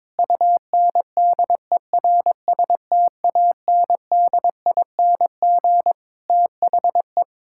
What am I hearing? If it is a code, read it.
Morse code, heard as a keyed tone, about 22 wpm, 700 Hz. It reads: UNDERSTANDING THE
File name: Sentences with 9-Word Phrases - 22wpm